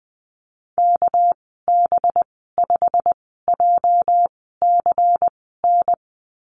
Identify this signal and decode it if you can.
Morse code, heard as a keyed tone, about 20 wpm, spaced as for 14 wpm, 700 Hz. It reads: KB5JCN
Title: 231 Lesson 37g - Letter x - Call Signs 20wpm